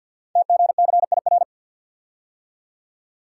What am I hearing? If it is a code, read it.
Morse code, heard as a keyed tone, about 51 wpm, 690 Hz. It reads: TGCIR
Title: Code Groups - 5 Letters - 50wpm